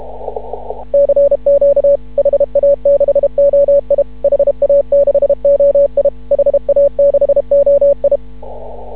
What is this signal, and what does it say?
Morse code, heard as a keyed tone, about 31 wpm, 560 Hz. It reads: CQ HA6OI HA6OI HA6OI
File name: CW / ha6oi 2